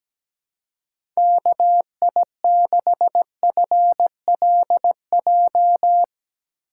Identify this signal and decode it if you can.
Morse code, heard as a keyed tone, about 17 wpm, 700 Hz. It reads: KI6FLJ
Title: USA Call Signs - 6 Characters - 6x Character Spacing - 17wpm